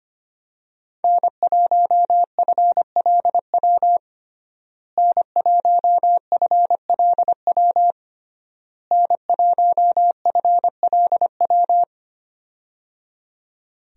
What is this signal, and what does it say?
Morse code, heard as a keyed tone, about 25 wpm, 700 Hz. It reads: N1FLW N1FLW N1FLW
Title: USA Call Signs - 5 Characters - Repeated 3x - 25wpm